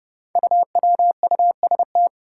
Morse code, heard as a keyed tone, about 30 wpm, 700 Hz. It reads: UWUST
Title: Five Challenging Letter-Number Combinations 30wpm